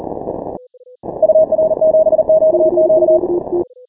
Morse code, heard as a keyed tone, about 40 wpm, 630 Hz. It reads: UR7QQ
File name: final / W1UJ-168-140-MorseRunner